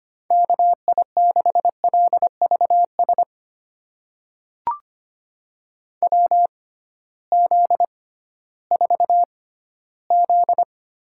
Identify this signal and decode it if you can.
Morse code, heard as a keyed tone, about 25 wpm, spaced as for 15 wpm, 700 Hz. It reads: KI6LVS WZ4Z
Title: USA Call Signs - 6x Character Spacing - 25wpm